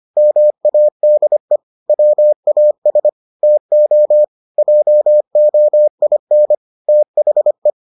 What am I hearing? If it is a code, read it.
Morse code, heard as a keyed tone, about 25 wpm, 600 Hz. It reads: MADE WAS TO JOIN THE